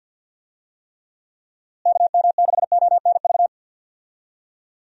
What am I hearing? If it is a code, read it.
Morse code, heard as a keyed tone, about 50 wpm, 700 Hz. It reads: KM6ONV
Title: USA Call Signs - 6 Characters - Repeated 3x - 50wpm